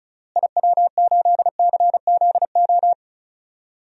Morse code, heard as a keyed tone, about 36 wpm, 700 Hz. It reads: IW8CZO
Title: DXCC-100 Call Signs - 6 Characters - Repeated 2x - 35wpm